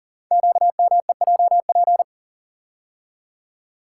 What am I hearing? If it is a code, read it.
Morse code, heard as a keyed tone, about 38 wpm, 700 Hz. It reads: QMEJP
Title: Code Groups - 5 Letter:Numbers - 40wpm